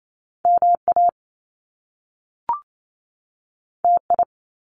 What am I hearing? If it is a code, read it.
Morse code, heard as a keyed tone, about 28 wpm, 700 Hz. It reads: MA TI
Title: Top Two Letter Combinations in Words 28wpm